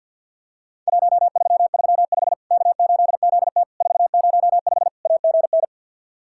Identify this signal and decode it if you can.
Morse code, heard as a keyed tone, about 54 wpm, 700 Hz. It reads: 1335 K8ZT 405 AGN
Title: Worked All Europe DX Contest - QTC Exchange with Repeats - 50wpm